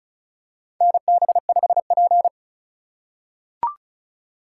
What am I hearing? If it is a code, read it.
Morse code, heard as a keyed tone, about 34 wpm, 700 Hz. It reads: NB5P
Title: USA Call Signs - 3x Character Spacing - 35wpm